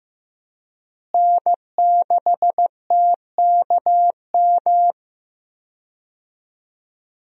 Morse code, heard as a keyed tone, about 15 wpm, 700 Hz. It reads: N6TKM
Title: USA Call Signs - 5 Characters - Repeated 2x - 15wpm